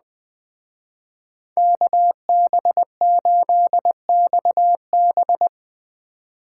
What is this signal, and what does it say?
Morse code, heard as a keyed tone, about 20 wpm, 700 Hz. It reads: KB8XB